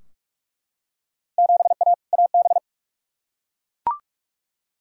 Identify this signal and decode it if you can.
Morse code, heard as a keyed tone, about 42 wpm, 700 Hz. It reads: 7A AB
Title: Field Day Exchanges Encore - 45wpm